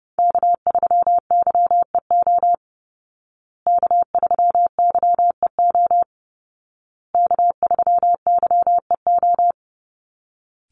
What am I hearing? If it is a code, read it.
Morse code, heard as a keyed tone, about 30 wpm, 700 Hz. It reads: K3YEO K3YEO K3YEO
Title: Call Signs Repeated 2x - 30wpm